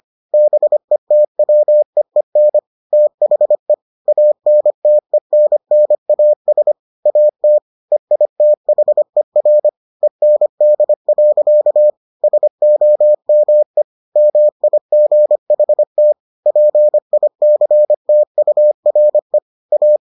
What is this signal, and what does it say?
Morse code, heard as a keyed tone, about 25 wpm, 600 Hz. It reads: BETWEEN THE ANTENNAS AT EITHER END. SOME MIGHT PICTURE A